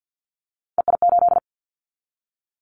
Morse code, heard as a keyed tone, about 50 wpm, 700 Hz. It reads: EI8
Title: Callsign Prefixes for Top 50 DXCC - 50wpm